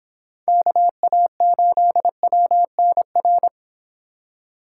Morse code, heard as a keyed tone, about 26 wpm, 700 Hz. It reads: KA8WNR